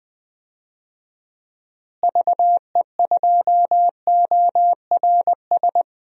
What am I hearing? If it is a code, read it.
Morse code, heard as a keyed tone, about 20 wpm, 700 Hz. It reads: VE2ORS
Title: DXCC-100 Call Signs - 6 Characters - Repeated 3x - 20wpm